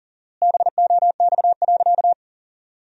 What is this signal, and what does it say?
Morse code, heard as a keyed tone, about 40 wpm, 700 Hz. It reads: BOX.
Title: 4x Word-Spacing - Sentences from Top 500 Words - 40wpm